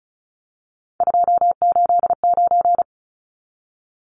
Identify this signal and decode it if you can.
Morse code, heard as a keyed tone, about 34 wpm, 700 Hz. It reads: 289
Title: Numbers - 3-Digit 35wpm